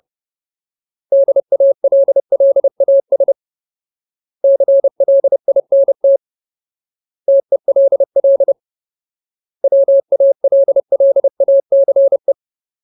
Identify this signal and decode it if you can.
Morse code, heard as a keyed tone, about 30 wpm, 550 Hz. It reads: DALLAS CLINT TELL WALLACE
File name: Ultimate Word List / 30WPM Ultimate